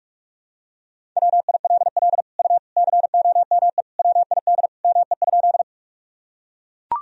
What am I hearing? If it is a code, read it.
Morse code, heard as a keyed tone, about 41 wpm, 700 Hz. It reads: WILL U COME WID ME?
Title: Sentences from Top 200 Words and Abbreviations 45wpm